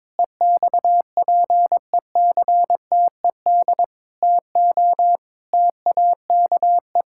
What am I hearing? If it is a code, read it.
Morse code, heard as a keyed tone, about 22 wpm, 700 Hz. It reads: EXPECTED TO TAKE
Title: Pangram Challenge v2 - 22wpm